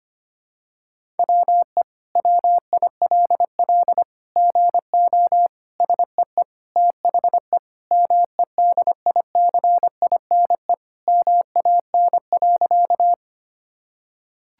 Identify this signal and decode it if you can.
Morse code, heard as a keyed tone, about 25 wpm, 700 Hz. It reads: WE WILL GO SEE THE MEDICINE MAN.
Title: Sentences from Top 1000 Words - Review Part 6 - 25wpm